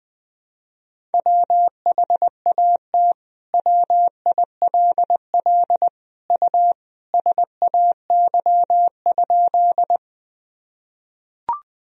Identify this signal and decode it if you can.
Morse code, heard as a keyed tone, about 20 wpm, 700 Hz. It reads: WHAT WILL U SAY?
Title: Sentences from Top 200 Words and Abbreviations 20wpm